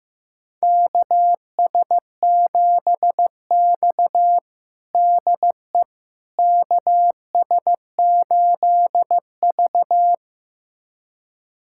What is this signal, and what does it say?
Morse code, heard as a keyed tone, about 15 wpm, 700 Hz. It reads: KS7X DE KS8V
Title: USA Callsign DE Callsign - 4 Characters - 15wpm